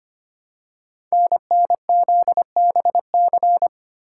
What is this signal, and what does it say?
Morse code, heard as a keyed tone, about 25 wpm, 700 Hz. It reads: NNZBC
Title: Code Groups - 5 Letters - 25wpm